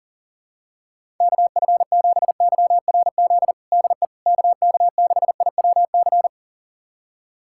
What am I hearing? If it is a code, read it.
Morse code, heard as a keyed tone, about 40 wpm, 700 Hz. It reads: KF7YRZ DE KK6IWC
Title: USA Callsign DE Callsign - 6 Characters - 40wpm